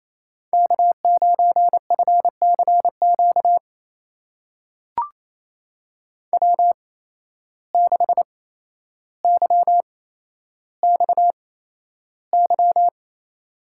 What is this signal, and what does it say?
Morse code, heard as a keyed tone, about 28 wpm, spaced as for 12 wpm, 700 Hz. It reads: K9FCQ W6YXY
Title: USA Call Signs - 5 Characters - 8x Character Spacing - 28wpm